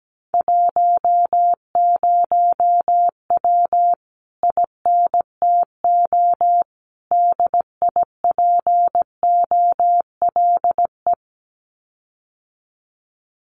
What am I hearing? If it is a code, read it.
Morse code, heard as a keyed tone, about 17 wpm, 700 Hz. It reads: 10W INTO DIPOLE
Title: Common QSO Phrases Encore 17wpm